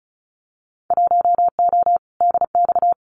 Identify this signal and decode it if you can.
Morse code, heard as a keyed tone, about 36 wpm, 700 Hz. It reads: 1O DX
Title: Winter Field Day Exchanges 35wpm